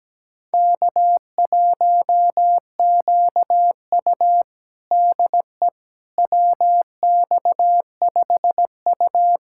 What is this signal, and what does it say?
Morse code, heard as a keyed tone, about 17 wpm, 700 Hz. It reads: K1QU DE WX5U
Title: USA Callsign DE Callsign - 4 Characters - 17wpm